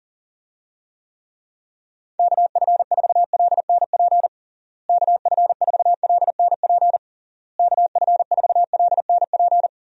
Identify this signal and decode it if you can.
Morse code, heard as a keyed tone, about 40 wpm, 700 Hz. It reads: KF4LNP KF4LNP KF4LNP
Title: USA Call Signs - 6 Characters - Repeated 3x - 40wpm